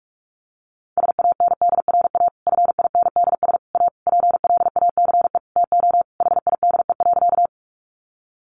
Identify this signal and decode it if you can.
Morse code, heard as a keyed tone, about 44 wpm, 700 Hz. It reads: SANDRA FINDS A PLACE TO HIDE.
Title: Easy Phrases 45wpm